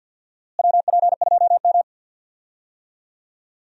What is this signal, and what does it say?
Morse code, heard as a keyed tone, about 49 wpm, 700 Hz. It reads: WP2M
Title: USA Call Signs - 4 Characters - Repeated 3x - 50wpm